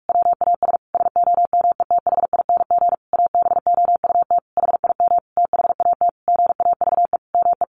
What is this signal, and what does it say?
Morse code, heard as a keyed tone, about 44 wpm, 700 Hz. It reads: WAS SOMETHING ABOUT HIM THAT GAVE ME
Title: Sentences from Top 1500 Words 45wpm